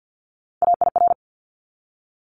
Morse code, heard as a keyed tone, about 50 wpm, 700 Hz. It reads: AIR
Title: Multi-letter Phonemes 50wpm